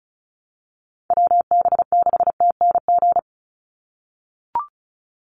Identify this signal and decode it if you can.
Morse code, heard as a keyed tone, about 34 wpm, 700 Hz. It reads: WB6TNG
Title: USA Call Signs 35wpm